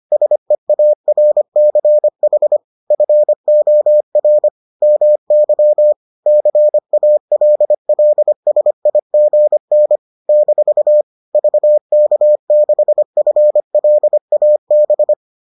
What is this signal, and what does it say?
Morse code, heard as a keyed tone, about 25 wpm, 600 Hz. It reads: SEARCH FOR MY CALLSIGN - VK6FLAB